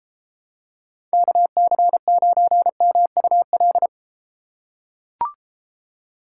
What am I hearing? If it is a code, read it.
Morse code, heard as a keyed tone, about 32 wpm, 700 Hz. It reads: KC9MUL